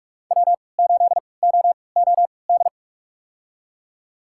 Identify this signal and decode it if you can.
Morse code, heard as a keyed tone, about 44 wpm, spaced as for 25 wpm, 700 Hz. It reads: W8OOD
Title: USA Call Signs - 5 Characters - 3x Character Spacing - 45wpm